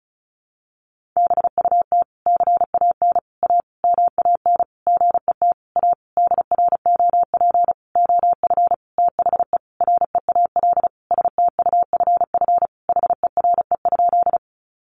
Morse code, heard as a keyed tone, about 34 wpm, 700 Hz. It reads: BUT CAN A MAN GET A DROP OF THE REAL STUFF HERE?
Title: Sentences from Top 1400 Words 35wpm